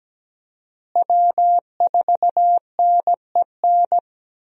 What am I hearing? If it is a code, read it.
Morse code, heard as a keyed tone, about 17 wpm, 700 Hz. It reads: W4NEN
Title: USA Call Signs - 5 Characters - Repeated 2x - 17wpm